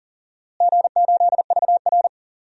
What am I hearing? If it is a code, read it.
Morse code, heard as a keyed tone, about 38 wpm, 700 Hz. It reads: G8VR
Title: DXCC-100 Call Signs - 4 Characters - Repeated 3x - 40wpm